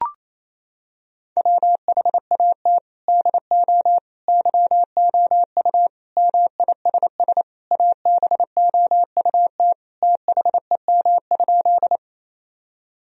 Morse code, heard as a keyed tone, about 28 wpm, 700 Hz. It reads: WHAT DO YOU MISS ABOUT THEM?
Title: Sentences from Top 1000 Words - Review Part 3 - 28wpm